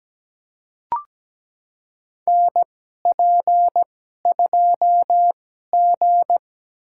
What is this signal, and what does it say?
Morse code, heard as a keyed tone, about 17 wpm, spaced as for 12 wpm, 700 Hz. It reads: NP2G
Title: USA Call Signs - 4 Characters - 2x Character Spacing - 17wpm